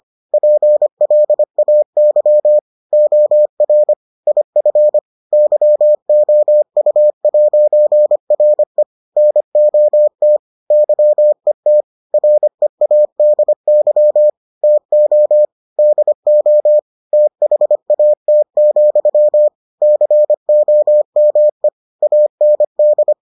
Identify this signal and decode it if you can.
Morse code, heard as a keyed tone, about 25 wpm, 600 Hz. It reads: PLAY OR IF YOU'RE NOT YET READY TO DO THAT, COME AND